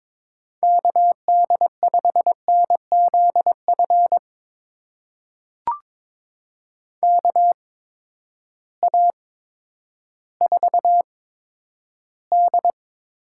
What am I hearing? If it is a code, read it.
Morse code, heard as a keyed tone, about 22 wpm, spaced as for 11 wpm, 700 Hz. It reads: KD5NZF KA4D
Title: USA Call Signs - 8x Character Spacing - 22wpm